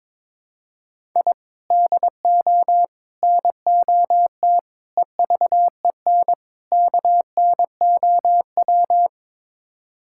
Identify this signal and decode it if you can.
Morse code, heard as a keyed tone, about 22 wpm, 700 Hz. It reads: I DO NOT EVEN KNOW
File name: practice / 22wpm / Build A Phrase - 5-Word Phrases - 22wpm